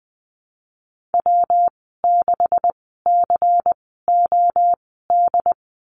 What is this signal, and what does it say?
Morse code, heard as a keyed tone, about 19 wpm, spaced as for 14 wpm, 700 Hz. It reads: W6COD
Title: 161 Lesson 26g - Letter g - Call Signs 20wpm